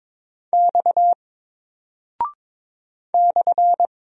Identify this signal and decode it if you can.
Morse code, heard as a keyed tone, about 23 wpm, 700 Hz. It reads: X /
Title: Single Character (Letter:Number:Punctuation:Prosign) 22wpm